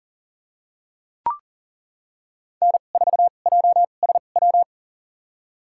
Morse code, heard as a keyed tone, about 37 wpm, spaced as for 28 wpm, 700 Hz. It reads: N4JSW
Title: USA Call Signs - 5 Characters - 2x Character Spacing - 40wpm